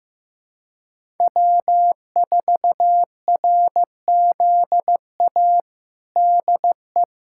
Morse code, heard as a keyed tone, about 14 wpm, 700 Hz. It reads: W4RZA DE
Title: Practice QSO - Exchange 4 - Ending - 15wpm